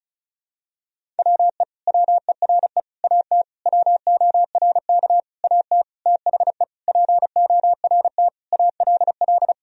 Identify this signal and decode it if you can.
Morse code, heard as a keyed tone, about 36 wpm, 700 Hz. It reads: WE WERE AT WORK AT THE PORT ALL
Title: Sentences from Top 1000 Words - Review Part 1 - 35wpm